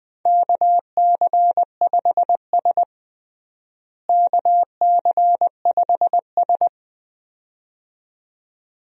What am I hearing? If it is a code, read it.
Morse code, heard as a keyed tone, about 20 wpm, 700 Hz. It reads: KC5S KC5S
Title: USA Call Signs - 4 Characters - Repeated 2x - 20wpm